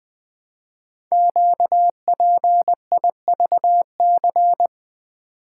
Morse code, heard as a keyed tone, about 20 wpm, 700 Hz. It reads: QPIVC
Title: Code Groups - 5 Letters - 20wpm